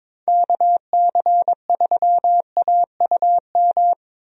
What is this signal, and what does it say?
Morse code, heard as a keyed tone, about 22 wpm, 700 Hz. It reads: KC3AUM
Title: USA Call Signs - 6 Characters - Repeated 2x - 22wpm